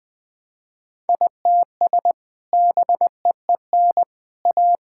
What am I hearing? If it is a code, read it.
Morse code, heard as a keyed tone, about 21 wpm, 700 Hz. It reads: ITS BEEN A
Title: Sentences with 3-Word Phrases - 20wpm